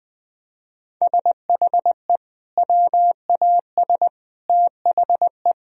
Morse code, heard as a keyed tone, about 21 wpm, 700 Hz. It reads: SHE WAS THE
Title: Sentences from Top 1000 Words - Review Part 3 - 20wpm